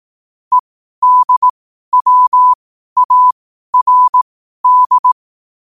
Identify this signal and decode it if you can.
Morse code, heard as a keyed tone, about 18 wpm, spaced as for 12 wpm, 1000 Hz. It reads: EDWARD
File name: qotdfiles / 12wpm